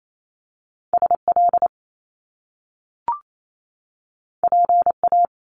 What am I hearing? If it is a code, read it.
Morse code, heard as a keyed tone, about 28 wpm, 700 Hz. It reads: SL PA
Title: Top Two Letter Combinations in Words 28wpm